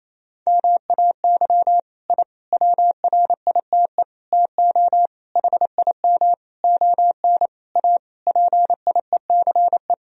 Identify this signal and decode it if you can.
Morse code, heard as a keyed tone, about 28 wpm, 700 Hz. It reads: MAY I WRITE TO HIM ON A PIECE
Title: Sentences from Top 1000 Words - Review Part 2 - 28wpm